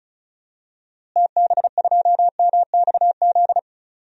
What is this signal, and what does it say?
Morse code, heard as a keyed tone, about 34 wpm, 700 Hz. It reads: TB2MXZ
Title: DXCC-100 Call Signs - 6 Characters - Repeated 2x - 35wpm